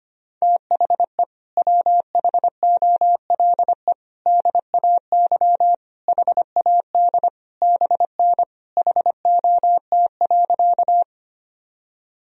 Morse code, heard as a keyed tone, about 25 wpm, 700 Hz. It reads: THE WHOLE DAY HAD BN HOT.
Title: Sentences from Top 500 Words and Abbreviations 25wpm